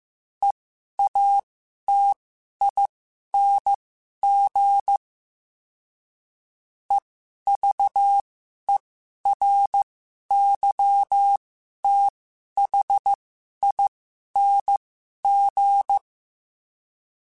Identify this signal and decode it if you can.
Morse code, heard as a keyed tone, about 15 wpm, spaced as for 11 wpm, 800 Hz. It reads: EATING EVERYTHING
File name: Special INGs-mp3s / Special INGs-mp3s / Sngl Ing @10